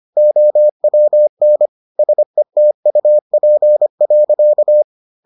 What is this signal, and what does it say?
Morse code, heard as a keyed tone, about 25 wpm, 600 Hz. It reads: OWN SETUP.